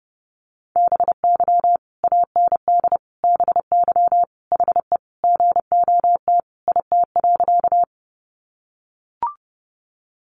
Morse code, heard as a keyed tone, about 28 wpm, 700 Hz. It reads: BY AND BY HE GOT IT.
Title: Sentences from Top 300 Words 30wpm